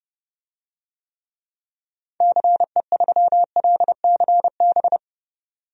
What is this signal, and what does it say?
Morse code, heard as a keyed tone, about 32 wpm, 700 Hz. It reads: CE3LCB
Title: DXCC-100 Call Signs - 6 Characters - 30wpm